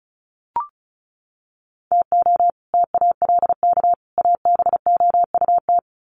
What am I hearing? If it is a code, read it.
Morse code, heard as a keyed tone, about 34 wpm, 700 Hz. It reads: TO TALK ABOUT
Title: Sets of 3 Words 35wpm